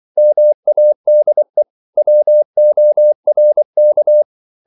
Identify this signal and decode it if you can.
Morse code, heard as a keyed tone, about 24 wpm, 600 Hz. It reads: MADE WORK
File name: CW / Practice / 500MostCommonEnglishWords24